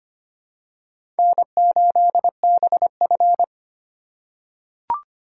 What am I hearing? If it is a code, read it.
Morse code, heard as a keyed tone, about 25 wpm, 700 Hz. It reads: N8BF
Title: USA Call Signs - 4 Characters - 4x Character Spacing - 25wpm